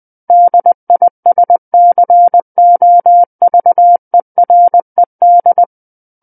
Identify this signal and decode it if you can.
Morse code, heard as a keyed tone, about 20 wpm, 700 Hz. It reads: DISCOVERED
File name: 20 WPM